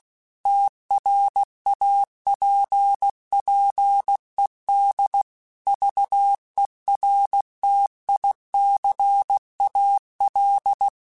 Morse code, heard as a keyed tone, about 16 wpm, 800 Hz. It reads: TRAPPED VERTICAL
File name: QSOs / QSOs-mp3s / WA7QOX 2@15